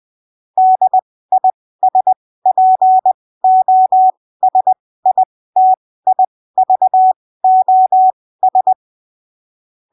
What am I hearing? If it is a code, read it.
Morse code, heard as a keyed tone, about 20 wpm, spaced as for 15 wpm, 750 Hz. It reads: DISPOSITIVOS
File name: arquivos / texto / texto 03 20ppm